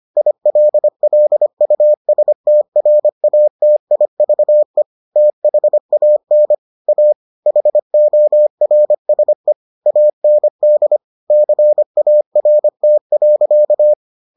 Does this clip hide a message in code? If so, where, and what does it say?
Yes: Morse code from 0.2 to 13.9 s, heard as a keyed tone, about 25 wpm, 600 Hz. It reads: ILLUSTRATIVE THAN A HORSE AND CART.